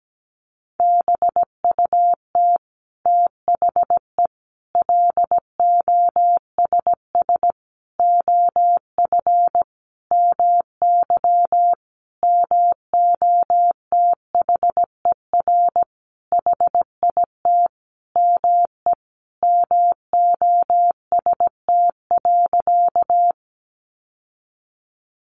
Morse code, heard as a keyed tone, about 17 wpm, 700 Hz. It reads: BUT THE LOSS OF MY MOTHER HIT ME MOST.
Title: Sentences from Top 1400 Words 17wpm